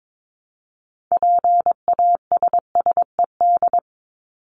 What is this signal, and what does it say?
Morse code, heard as a keyed tone, about 22 wpm, 700 Hz. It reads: PASSED
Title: ED Suffix - Top 100 Words with Base Words 22wpm